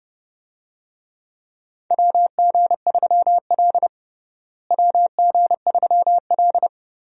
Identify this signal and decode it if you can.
Morse code, heard as a keyed tone, about 30 wpm, 700 Hz. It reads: WG3L WG3L
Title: USA Call Signs - 4 Characters - Repeated 3x - 30wpm